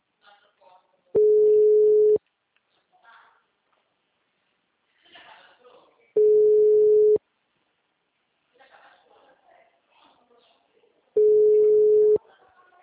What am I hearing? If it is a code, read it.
Ringback tone (Europe, 425 Hz), with speech.
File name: out- -2012- - - .1662